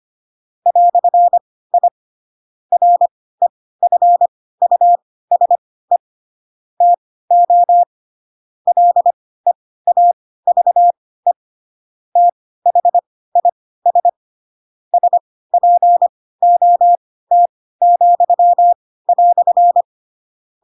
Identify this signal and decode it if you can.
Morse code, heard as a keyed tone, about 25 wpm, spaced as for 16 wpm, 700 Hz. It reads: "I REFUSE TO LEAVE THIS SPOT,"
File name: Pinocchio / Pinocchio ch4 10 25 ik1hge